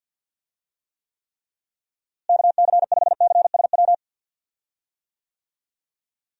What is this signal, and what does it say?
Morse code, heard as a keyed tone, about 47 wpm, 700 Hz. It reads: KC5CSW
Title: USA Call Signs - 6 Characters - Repeated 3x - 50wpm